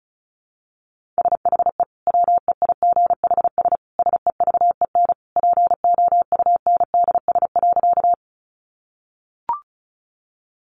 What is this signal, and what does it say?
Morse code, heard as a keyed tone, about 34 wpm, 700 Hz. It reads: SHE WEIGHS SEVEN POUNDS.
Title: Easy Phrases 35wpm